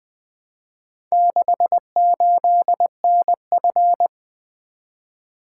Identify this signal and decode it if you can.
Morse code, heard as a keyed tone, about 20 wpm, 700 Hz. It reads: 68NF
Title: Four Letter-Number Combinations - 20wpm